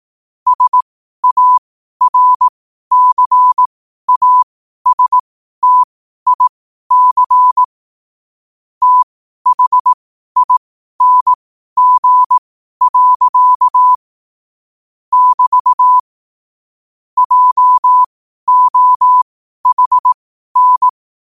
Morse code, heard as a keyed tone, about 18 wpm, spaced as for 12 wpm, 1000 Hz. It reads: SARCASTIC THING. <BT> JOHN
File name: qotdfiles / 12wpm